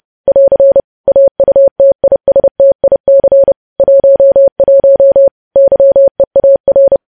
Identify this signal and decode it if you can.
Morse code, heard as a keyed tone, about 30 wpm, 550 Hz. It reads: <AR> AUTISTIC 11 YEAR
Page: news headlines in morse code at 30 wpm